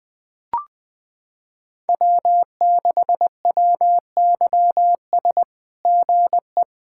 Morse code, heard as a keyed tone, about 20 wpm, 700 Hz. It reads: W6WYS GE
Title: SOTA QSO Practice 20wpm